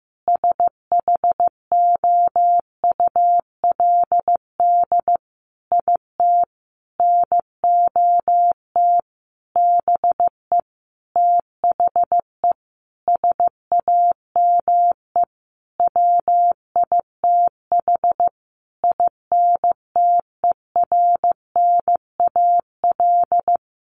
Morse code, heard as a keyed tone, about 15 wpm, 700 Hz. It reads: SHOULD IT NOT BE THE SAME WITH INTERNAL
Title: Sentences from Top 1400 Words 15wpm